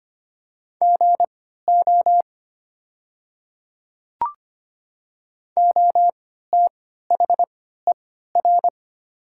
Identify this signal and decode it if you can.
Morse code, heard as a keyed tone, about 25 wpm, spaced as for 14 wpm, 700 Hz. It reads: GO OTHER
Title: 3x Character Spacing - Top 100 Words - 25wpm